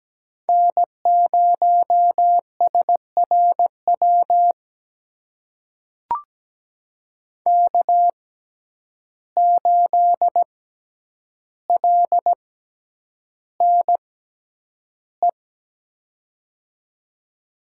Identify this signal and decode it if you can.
Morse code, heard as a keyed tone, about 17 wpm, spaced as for 9 wpm, 700 Hz. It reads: N0SRW K8LNE
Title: USA Call Signs - 5 Characters - 6x Character Spacing - 17wpm